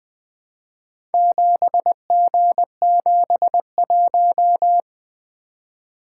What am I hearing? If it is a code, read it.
Morse code, heard as a keyed tone, about 20 wpm, 700 Hz. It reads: 7G71
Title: Four Letter-Number Combinations - 20wpm